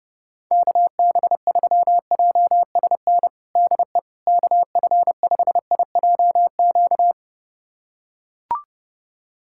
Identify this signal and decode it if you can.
Morse code, heard as a keyed tone, about 30 wpm, 700 Hz. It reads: KB3JSN DE KF5IJQ
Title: USA Callsign DE Callsign - 6 Characters - 30wpm